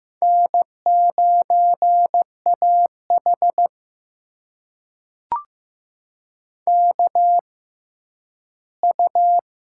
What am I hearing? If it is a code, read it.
Morse code, heard as a keyed tone, about 15 wpm, spaced as for 10 wpm, 700 Hz. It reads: N9AH KU
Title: USA Call Signs - 4 Characters - 6x Character Spacing - 15wpm